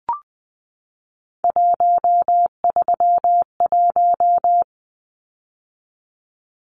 Morse code, heard as a keyed tone, about 20 wpm, 700 Hz. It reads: 131
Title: Numbers - 3-Digit 20wpm